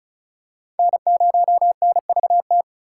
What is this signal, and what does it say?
Morse code, heard as a keyed tone, about 34 wpm, 700 Hz. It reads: N0NVT
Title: USA Call Signs - 5 Characters - 4x Character Spacing - 35wpm